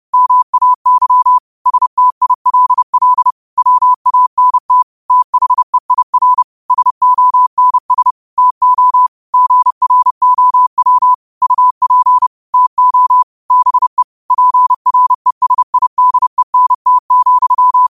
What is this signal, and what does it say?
Morse code, heard as a keyed tone, about 30 wpm, 1000 Hz. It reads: MAY STILL WANT THEIR SONS TO GROW UP TO BE PRESIDENT,
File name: qotdfiles / 30wpm